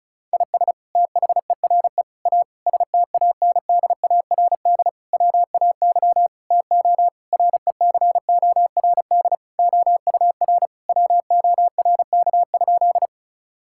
Morse code, heard as a keyed tone, about 36 wpm, 700 Hz. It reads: IS THERE A STANDARD WAY TO RECORD OUR WORK?
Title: Sentences from Top 1000 Words - Review Part 6 - 35wpm